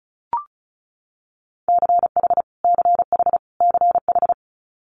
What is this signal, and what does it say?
Morse code, heard as a keyed tone, about 36 wpm, 700 Hz. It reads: CH CH CH
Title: Multi-letter Phonemes Repeated 3x 35wpm